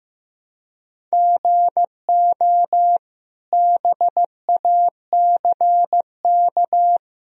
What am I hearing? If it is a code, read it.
Morse code, heard as a keyed tone, about 15 wpm, 700 Hz. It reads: GO BACK
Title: Sentences from Top 1000 Words - Review Part 1 - 15wpm